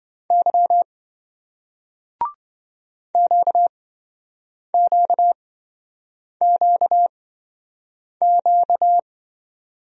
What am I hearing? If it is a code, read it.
Morse code, heard as a keyed tone, about 27 wpm, spaced as for 7 wpm, 700 Hz. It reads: Y QQQQ